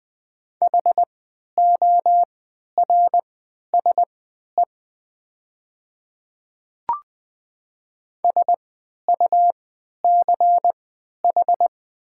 Morse code, heard as a keyed tone, about 20 wpm, spaced as for 11 wpm, 700 Hz. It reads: HORSE SUCH
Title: 3x Character Spacing - Top 300 Words - 20wpm